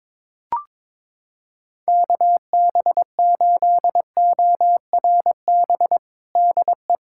Morse code, heard as a keyed tone, about 22 wpm, 700 Hz. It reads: KB8ORB DE
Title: USA Callsign DE Callsign - 6 Characters - 22wpm